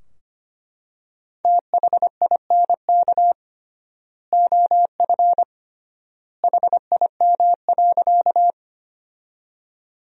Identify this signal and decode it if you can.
Morse code, heard as a keyed tone, about 25 wpm, 700 Hz. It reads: THINK OF HIM.
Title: 3x Word-Spacing - Sentences from Top 200 Words - 25wpm